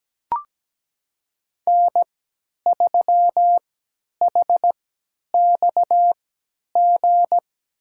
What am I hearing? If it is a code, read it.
Morse code, heard as a keyed tone, about 17 wpm, spaced as for 10 wpm, 700 Hz. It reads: N3HXG
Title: USA Call Signs - 5 Characters - 3x Character Spacing - 17wpm